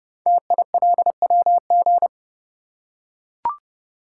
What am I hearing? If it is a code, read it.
Morse code, heard as a keyed tone, about 30 wpm, 700 Hz. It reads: TILWG
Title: Code Groups - 5 Letters - 30wpm